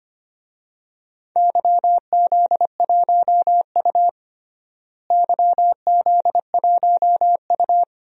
Morse code, heard as a keyed tone, about 25 wpm, 700 Hz. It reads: YZ1U YZ1U
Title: DXCC-100 Call Signs - 4 Characters - Repeated 2x - 25wpm